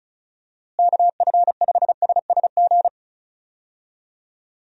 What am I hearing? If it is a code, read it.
Morse code, heard as a keyed tone, about 34 wpm, 700 Hz. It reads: KF5SSG
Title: USA Call Signs - 6 Characters - Repeated 3x - 35wpm